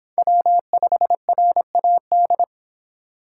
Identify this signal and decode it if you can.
Morse code, heard as a keyed tone, about 26 wpm, 700 Hz. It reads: W5RAD